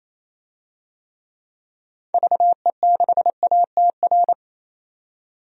Morse code, heard as a keyed tone, about 28 wpm, 700 Hz. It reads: VE6ATR
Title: DXCC-100 Call Signs - 6 Characters - 28wpm